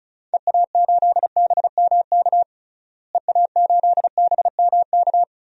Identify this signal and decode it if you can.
Morse code, heard as a keyed tone, about 36 wpm, 700 Hz. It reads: EA8BMK EA8BMK
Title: DXCC-100 Call Signs - 6 Characters - Repeated 2x - 35wpm